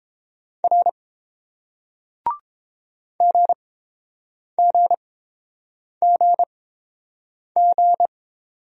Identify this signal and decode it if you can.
Morse code, heard as a keyed tone, about 30 wpm, spaced as for 7 wpm, 700 Hz. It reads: R GGGG